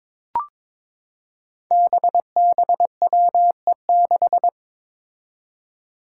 Morse code, heard as a keyed tone, about 21 wpm, 700 Hz. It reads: BBWE6
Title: Five Challenging Letter-Number Combinations 22wpm